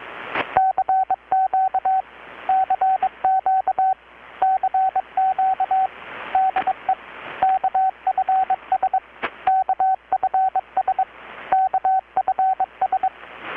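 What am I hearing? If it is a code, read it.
Morse code, heard as a keyed tone, about 22 wpm, 760 Hz, with radio noise. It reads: CQ CQ CQ DE KFS KFS KFS
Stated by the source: Morse from KPH's Night of Nights XX maritime radiotelegraphy event, received in Chiapas, Mexico about 3,800 km away on a Kenwood R-600 with a grounded 10 m long-wire antenna.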